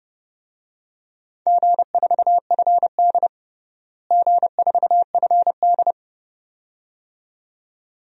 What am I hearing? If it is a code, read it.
Morse code, heard as a keyed tone, about 30 wpm, 700 Hz. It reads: G4FD G4FD
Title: DXCC-100 Call Signs - 4 Characters - Repeated 2x - 30wpm